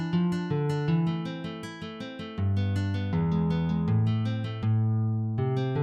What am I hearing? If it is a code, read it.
There is no code.